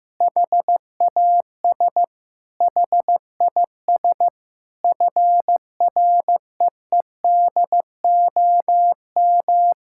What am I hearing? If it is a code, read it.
Morse code, heard as a keyed tone, about 15 wpm, 700 Hz. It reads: HAS HIS FREEDOM
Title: Sentences from Top 1000 Words - Review Part 5 - 15wpm